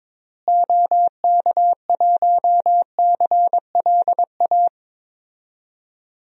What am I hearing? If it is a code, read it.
Morse code, heard as a keyed tone, about 22 wpm, 700 Hz. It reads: OK1CLA
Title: DXCC-100 Call Signs - 6 Characters - Repeated 3x - 22wpm